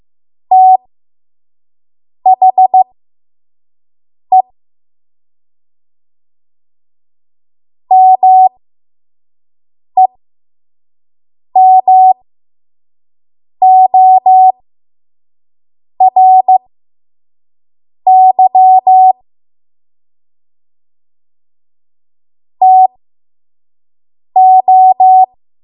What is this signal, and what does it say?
Morse code, heard as a keyed tone, about 15 wpm, spaced as for 5 wpm, 750 Hz. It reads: THE MEMORY TO
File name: Morse / 05WPM